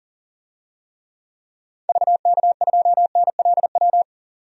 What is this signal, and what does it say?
Morse code, heard as a keyed tone, about 40 wpm, 700 Hz. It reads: VK2NLW
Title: DXCC-100 Call Signs - 6 Characters - 40wpm